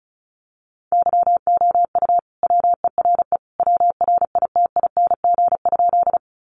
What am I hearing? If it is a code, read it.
Morse code, heard as a keyed tone, about 34 wpm, 700 Hz. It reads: YOU WERE WRITING?
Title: Sentences from Top 800 Words 35wpm